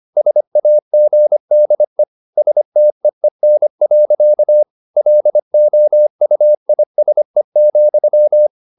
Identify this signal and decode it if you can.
Morse code, heard as a keyed tone, about 25 wpm, 600 Hz. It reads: SAGDE STEEN. LOUISE,
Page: Ferie 25 wpm